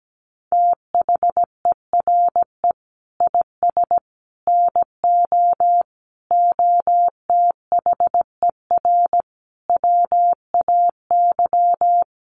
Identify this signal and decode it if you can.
Morse code, heard as a keyed tone, about 17 wpm, 700 Hz. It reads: THERE IS NO OTHER WAY
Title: Sentences from Top 1300 Words 17wpm